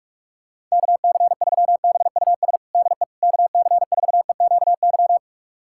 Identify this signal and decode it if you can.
Morse code, heard as a keyed tone, about 46 wpm, 700 Hz. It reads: KC3BUS DE KC4EQY
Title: USA Callsign DE Callsign 45wpm